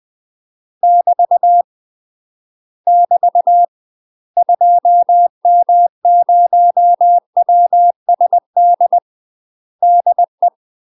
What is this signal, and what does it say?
Morse code, heard as a keyed tone, about 20 wpm, 700 Hz. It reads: <BT> <BT> 2M0WSD DE